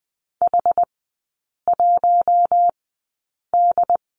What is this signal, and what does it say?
Morse code, heard as a keyed tone, about 20 wpm, spaced as for 8 wpm, 700 Hz. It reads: H1D
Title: 093 Lesson 16e - Letter c - Three Characters 20wpm